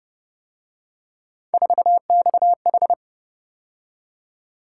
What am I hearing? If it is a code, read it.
Morse code, heard as a keyed tone, about 30 wpm, 700 Hz. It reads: 4XH